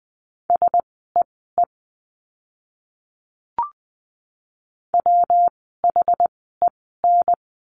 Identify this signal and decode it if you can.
Morse code, heard as a keyed tone, about 20 wpm, spaced as for 15 wpm, 700 Hz. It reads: SEE WHEN
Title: 139 Lesson 23f - Letter f - Words 20wpm